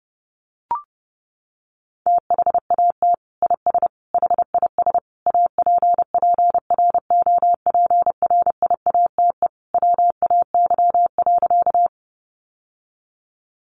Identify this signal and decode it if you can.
Morse code, heard as a keyed tone, about 30 wpm, 700 Hz. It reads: THAT IS HIS APPROPRIATE WAY.
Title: Sentences from Top 900 Words 30wpm